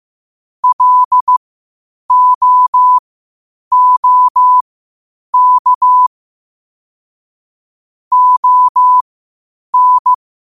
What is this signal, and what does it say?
Morse code, heard as a keyed tone, about 15 wpm, spaced as for 8 wpm, 1000 Hz. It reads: LOOK ON
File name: 8wpm